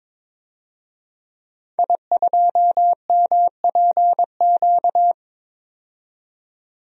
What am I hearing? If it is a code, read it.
Morse code, heard as a keyed tone, about 22 wpm, 700 Hz. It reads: I2MPQ
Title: DXCC-100 Call Signs - 5 Characters - 22wpm